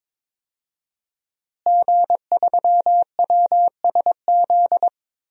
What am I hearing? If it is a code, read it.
Morse code, heard as a keyed tone, about 22 wpm, 700 Hz. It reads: G3WSZ
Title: DXCC-100 Call Signs - 5 Characters - Repeated 3x - 22wpm